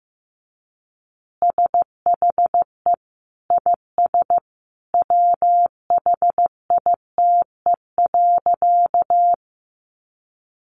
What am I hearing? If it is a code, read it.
Morse code, heard as a keyed tone, about 15 wpm, 700 Hz. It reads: SHE IS WHITE.
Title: Easy Phrases 15wpm